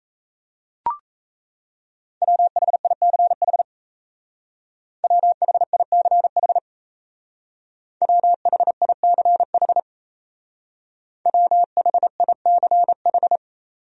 Morse code, heard as a keyed tone, about 35 wpm, 700 Hz. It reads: WHICH WHICH WHICH WHICH
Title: Speed-Racing Top 100 Words 28wpm